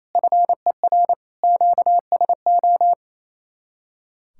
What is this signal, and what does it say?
Morse code, heard as a keyed tone, about 35 wpm, 700 Hz. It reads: FER QSO
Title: Speed-Racing Common QSO Phrases Encore - 28wpm